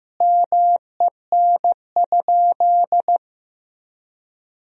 Morse code, heard as a keyed tone, about 15 wpm, 700 Hz. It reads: MEN?
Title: Sentences from Top 1000 Words - Review Part 3 - 15wpm